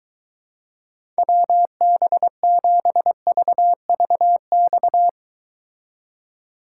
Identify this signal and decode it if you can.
Morse code, heard as a keyed tone, about 23 wpm, 700 Hz. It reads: WB7VVX